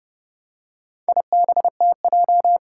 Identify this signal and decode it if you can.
Morse code, heard as a keyed tone, about 30 wpm, 700 Hz. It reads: IBTJ
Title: Four Challenging Letter-Number Combinations 30wpm